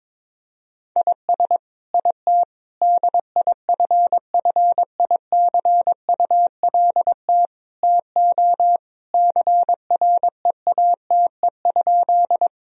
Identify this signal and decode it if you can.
Morse code, heard as a keyed tone, about 22 wpm, 700 Hz. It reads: IS IT DIFFICULT TO CREATE?
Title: Sentences from Top 1000 Words - Review Part 3 - 22wpm